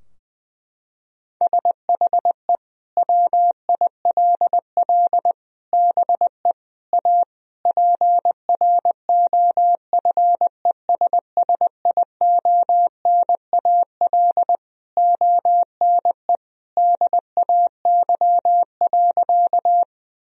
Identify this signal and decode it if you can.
Morse code, heard as a keyed tone, about 20 wpm, 700 Hz. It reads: SHE WILL BE A PROFESSIONAL ONE DAY.
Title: Sentences from Top 1000 Words - Review Part 4 - 20wpm